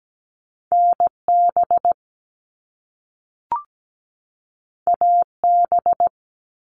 Morse code, heard as a keyed tone, about 17 wpm, 700 Hz. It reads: NB AB
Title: Canadian Provinces and Territories 17wpm